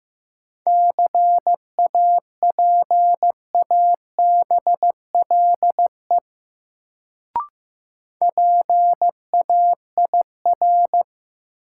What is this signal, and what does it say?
Morse code, heard as a keyed tone, about 15 wpm, 700 Hz. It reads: CAPABLE PAIR
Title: Sentences from Top 1200 Words - New Word Drill Alt - 15wpm